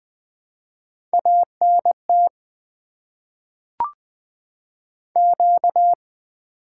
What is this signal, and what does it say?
Morse code, heard as a keyed tone, about 20 wpm, 700 Hz. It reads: A N T Q
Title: Top 20 QSO Elements - 8x Character Spacing - 20wpm